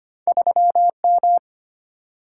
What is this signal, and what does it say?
Morse code, heard as a keyed tone, about 25 wpm, 700 Hz. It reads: 3M
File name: Two Letter-Number Combinations - 25wpm